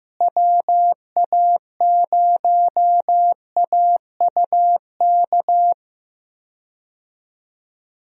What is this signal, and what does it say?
Morse code, heard as a keyed tone, about 15 wpm, 700 Hz. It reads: WA0AUK
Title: USA Call Signs - 6 Characters - Repeated 2x - 15wpm